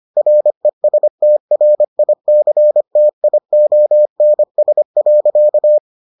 Morse code, heard as a keyed tone, about 25 wpm, 600 Hz. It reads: RESTRICTIONS.